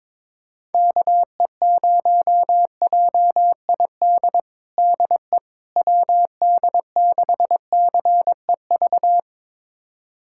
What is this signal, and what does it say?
Morse code, heard as a keyed tone, about 21 wpm, 700 Hz. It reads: KE0JID DE WD6CEV
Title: USA Callsign DE Callsign - 6 Characters - 22wpm